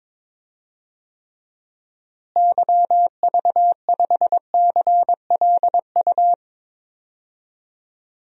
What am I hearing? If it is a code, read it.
Morse code, heard as a keyed tone, about 21 wpm, 700 Hz. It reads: YV5CLU
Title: DXCC-100 Call Signs - 6 Characters - Repeated 2x - 22wpm